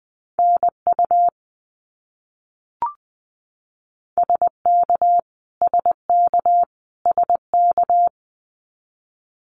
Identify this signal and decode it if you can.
Morse code, heard as a keyed tone, about 20 wpm, 700 Hz. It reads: NU SK SK SK
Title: Canadian Provinces and Territories - Repeated 3x - 20wpm